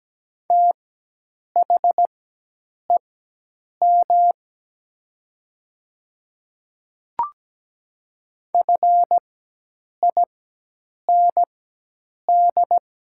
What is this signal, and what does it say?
Morse code, heard as a keyed tone, about 17 wpm, spaced as for 8 wpm, 700 Hz. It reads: THEM FIND
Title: 4x Character Spacing - Top 200 Words - 17wpm